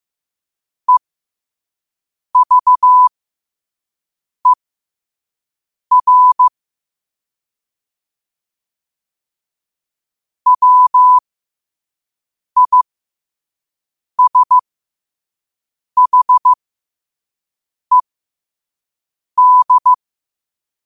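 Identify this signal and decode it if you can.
Morse code, heard as a keyed tone, about 15 wpm, spaced as for 5 wpm, 1000 Hz. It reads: EVER WISHED
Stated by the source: Morse code at 5 wpm.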